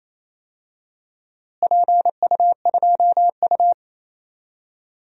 Morse code, heard as a keyed tone, about 28 wpm, 700 Hz. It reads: PU2U
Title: DXCC-100 Call Signs - 4 Characters - Repeated 3x - 28wpm